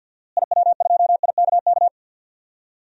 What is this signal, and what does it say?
Morse code, heard as a keyed tone, about 46 wpm, 700 Hz. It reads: IW2IKK
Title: DXCC-100 Call Signs - 6 Characters - Repeated 2x - 50wpm